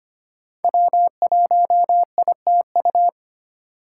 Morse code, heard as a keyed tone, about 25 wpm, 700 Hz. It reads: W1ITU
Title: USA Call Signs - 4x Character Spacing - 25wpm